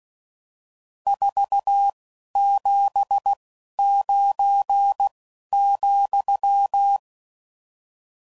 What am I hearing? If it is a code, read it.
Morse code, heard as a keyed tone, about 16 wpm, 800 Hz. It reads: 4 7 9 ,